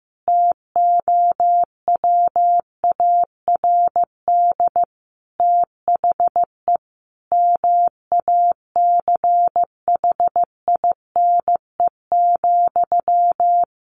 Morse code, heard as a keyed tone, about 15 wpm, 700 Hz. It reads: TOWARD THE MACHINE,
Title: A Sound Of Thunder 15wpm